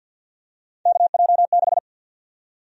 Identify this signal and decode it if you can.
Morse code, heard as a keyed tone, about 53 wpm, 700 Hz. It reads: KJ6
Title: USA Call Sign Prefixes - 50wpm